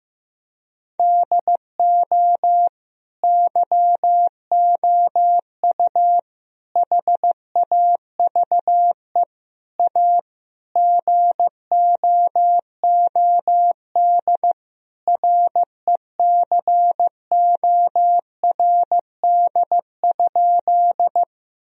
Morse code, heard as a keyed tone, about 15 wpm, 700 Hz. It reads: DO YOU HAVE A GOOD RECORD?
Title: Sentences from Top 1000 Words - Review Part 3 - 15wpm